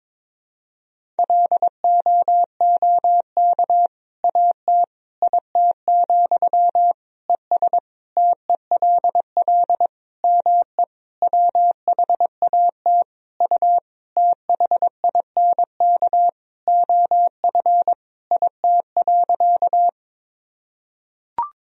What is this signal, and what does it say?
Morse code, heard as a keyed tone, about 22 wpm, 700 Hz. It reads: LOOK AT IT, ES TELL ME WHAT U THINK OF IT.
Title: Sentences from Top 200 Words and Abbreviations 22wpm